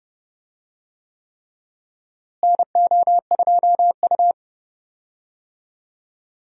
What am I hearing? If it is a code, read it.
Morse code, heard as a keyed tone, about 30 wpm, 700 Hz. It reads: NO2U
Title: USA Call Signs - 4 Characters - Repeated 3x - 30wpm